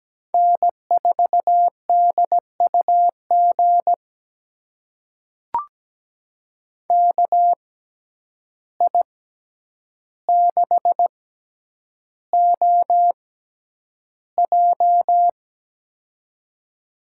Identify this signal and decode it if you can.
Morse code, heard as a keyed tone, about 17 wpm, spaced as for 9 wpm, 700 Hz. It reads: N4DUG KI6OJ
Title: USA Call Signs - 5 Characters - 6x Character Spacing - 17wpm